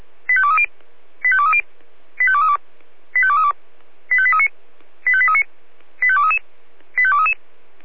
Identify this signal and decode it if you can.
Five-tone selective call (ZVEI), heard as quick runs of five tones, tones of about 70 ms. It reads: 85230 85230 85232 85232 85839 85839 85233 85233
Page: SDR/ZVEI